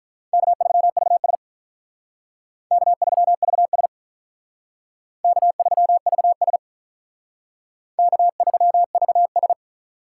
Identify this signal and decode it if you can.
Morse code, heard as a keyed tone, about 44 wpm, 700 Hz. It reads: K3VS K3VS K3VS K3VS